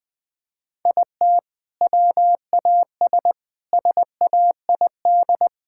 Morse code, heard as a keyed tone, about 20 wpm, 700 Hz. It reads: IT WAS SAID
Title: Sentences from Top 1600 Words 20wpm